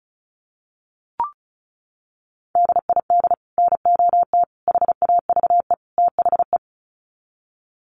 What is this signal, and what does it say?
Morse code, heard as a keyed tone, about 34 wpm, 700 Hz. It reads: DID NOT HAVE THE
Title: Sets of 4 Words 35wpm